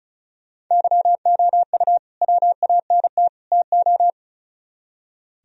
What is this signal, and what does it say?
Morse code, heard as a keyed tone, about 43 wpm, 700 Hz. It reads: YOU WANT TO
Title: Speed-Racing - Sets of 3 Words 35wpm